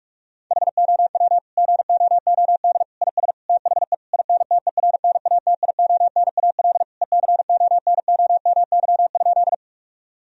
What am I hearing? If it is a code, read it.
Morse code, heard as a keyed tone, about 46 wpm, 700 Hz. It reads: HOW GOOD IS THE INTERNATIONAL ECONOMY?
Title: Sentences from Top 1000 Words - Review Part 4 - 45wpm